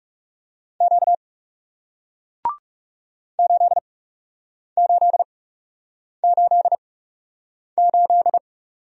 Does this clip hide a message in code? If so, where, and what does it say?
Yes: Morse code from 0.8 to 8.4 s, heard as a keyed tone, about 40 wpm, spaced as for 8 wpm, 700 Hz. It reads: Q 8888